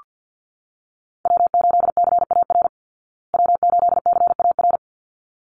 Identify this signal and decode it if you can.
Morse code, heard as a keyed tone, about 48 wpm, 700 Hz. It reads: W8CAR W8CAR
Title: Call Signs Repeated 3x - 50wpm